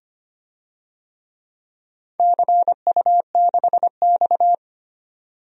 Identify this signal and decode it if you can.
Morse code, heard as a keyed tone, about 25 wpm, 700 Hz. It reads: CU6X
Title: DXCC-100 Call Signs - 4 Characters - 25wpm